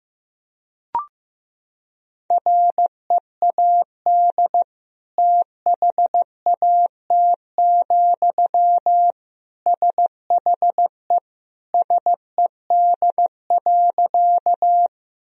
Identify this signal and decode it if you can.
Morse code, heard as a keyed tone, about 15 wpm, 700 Hz. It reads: READ THAT, SHE SED.
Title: Sentences from Top 200 Words and Abbreviations 15wpm